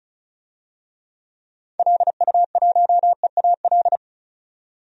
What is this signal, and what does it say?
Morse code, heard as a keyed tone, about 34 wpm, 700 Hz. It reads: LU1EAL
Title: DXCC-100 Call Signs - 6 Characters - 35wpm